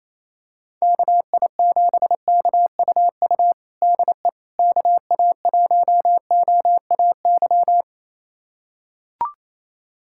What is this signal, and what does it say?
Morse code, heard as a keyed tone, about 28 wpm, 700 Hz. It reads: KI7KUU DE KA1OAY
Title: USA Callsign DE Callsign 28wpm